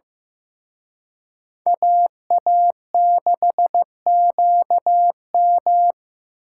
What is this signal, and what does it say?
Morse code, heard as a keyed tone, about 18 wpm, 700 Hz. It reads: AA6QM
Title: Speed Racing - USA 2x2 Call Signs 15wpm